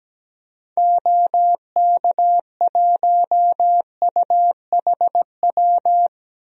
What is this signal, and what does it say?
Morse code, heard as a keyed tone, about 17 wpm, 700 Hz. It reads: OK1UHW
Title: DXCC-100 Call Signs - 6 Characters - Repeated 3x - 17wpm